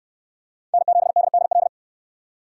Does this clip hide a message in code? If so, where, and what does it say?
Yes: Morse code from 0.7 to 1.7 s, heard as a keyed tone, about 65 wpm, 700 Hz. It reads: N7DDL